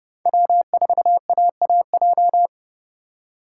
Morse code, heard as a keyed tone, about 30 wpm, 700 Hz. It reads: W4AAJ